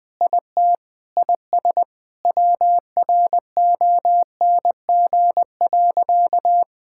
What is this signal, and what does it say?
Morse code, heard as a keyed tone, about 21 wpm, 700 Hz. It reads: IT IS WRONG.
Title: Sentences with 7-Word Phrases - Spoken Beginning - 20wpm